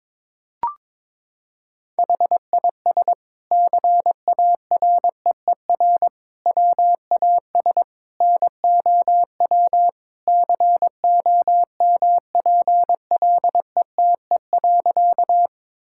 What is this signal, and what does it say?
Morse code, heard as a keyed tone, about 22 wpm, 700 Hz. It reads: HIS CAREER WAS NOW COMPLETE.
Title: Sentences from Top 1000 Words - Review Part 6 - 22wpm